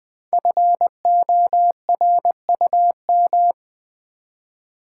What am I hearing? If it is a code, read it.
Morse code, heard as a keyed tone, about 20 wpm, 700 Hz. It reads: FORUM
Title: Build A Word - 1 Letter At A Time - 3 to 5-Letter Words - 20wpm